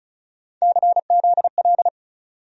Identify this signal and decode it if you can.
Morse code, heard as a keyed tone, about 34 wpm, 700 Hz. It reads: CZL
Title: USA Call Sign Postfixes - 3 Letters 35wpm